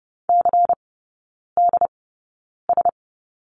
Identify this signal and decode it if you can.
Morse code, heard as a keyed tone, about 30 wpm, spaced as for 9 wpm, 700 Hz. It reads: CDS
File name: Lesson 16e - Letter c - Three Characters 30wpm